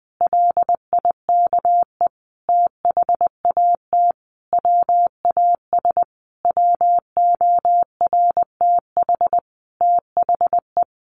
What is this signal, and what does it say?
Morse code, heard as a keyed tone, about 20 wpm, 700 Hz. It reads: LIKE THAT WAS WORTH THE
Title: Sentences from Top 1500 Words 20wpm